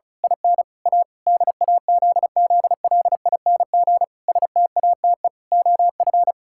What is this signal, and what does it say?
Morse code, heard as a keyed tone, about 36 wpm, 700 Hz. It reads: IN A DAZZLING STATE OF
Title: Pangram Challenge v2 - 35wpm